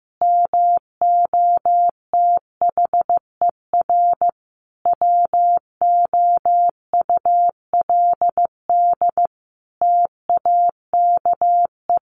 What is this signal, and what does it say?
Morse code, heard as a keyed tone, about 14 wpm, 700 Hz. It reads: MOTHER WOULD TAKE
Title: Sentences from Top 700 Words 15wpm